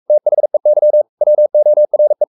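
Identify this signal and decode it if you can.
Morse code, heard as a keyed tone, about 44 wpm, 600 Hz. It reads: THEY WORE
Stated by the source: Morse code at 43 wpm sending Ken Follett's novel Never in English, from CW med Gnister.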